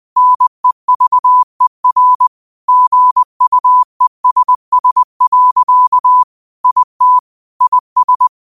Morse code, heard as a keyed tone, about 20 wpm, 1000 Hz. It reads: NEVER GUESS. IT IS